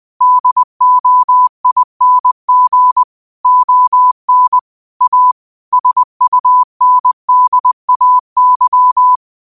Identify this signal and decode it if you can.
Morse code, heard as a keyed tone, about 20 wpm, 1000 Hz. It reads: DOING ON A SUNDAY